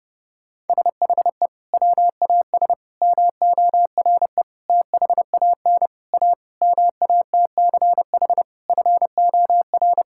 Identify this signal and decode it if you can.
Morse code, heard as a keyed tone, about 32 wpm, 700 Hz. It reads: SHE WAS MORE THAN A MATCH FOR
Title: Sentences from Top 1600 Words 30wpm